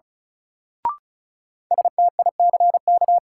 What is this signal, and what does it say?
Morse code, heard as a keyed tone, about 34 wpm, 700 Hz. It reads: STICK
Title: Sentences from Top 1200 Words - New Word Drill Alt - 35wpm